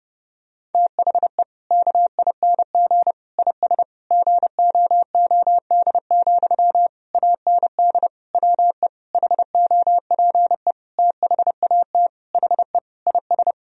Morse code, heard as a keyed tone, about 30 wpm, 700 Hz. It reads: THE KING IS GOOD, AND WE HOPE THAT HE IS
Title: Sentences from Top 1000 Words - Review Part 1 - 30wpm